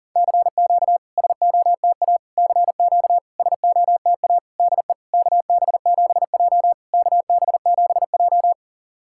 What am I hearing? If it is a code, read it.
Morse code, heard as a keyed tone, about 41 wpm, 690 Hz. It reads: CQ SOTA CQ SOTA DE KB7J KB7J
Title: SOTA QSO Practice 40wpm